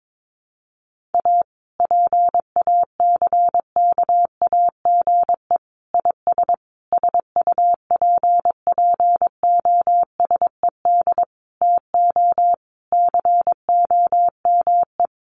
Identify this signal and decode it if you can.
Morse code, heard as a keyed tone, about 22 wpm, 700 Hz. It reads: A PACKAGE IS SUPPOSED TO COME
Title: Easy Phrases 22wpm